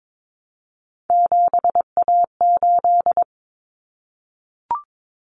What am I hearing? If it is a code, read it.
Morse code, heard as a keyed tone, about 22 wpm, 700 Hz. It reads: 7A8
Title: Common Cut Numbers - 3-Digit 22wpm